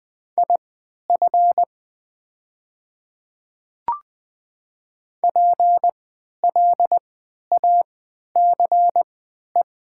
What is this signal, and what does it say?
Morse code, heard as a keyed tone, about 20 wpm, spaced as for 11 wpm, 700 Hz. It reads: IF PLACE
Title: 3x Character Spacing - Top 200 Words - 20wpm